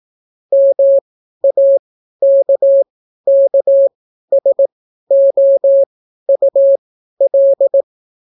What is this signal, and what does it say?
Morse code, heard as a keyed tone, about 18 wpm, spaced as for 12 wpm, 550 Hz. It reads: MAKKSOUL